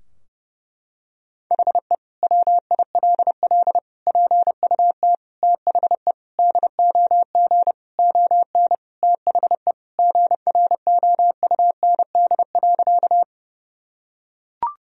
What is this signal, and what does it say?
Morse code, heard as a keyed tone, about 30 wpm, 700 Hz. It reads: HE WILL PUT THE DOG ON THE GROUND.
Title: Sentences from Top 1000 Words - Review Part 2 - 30wpm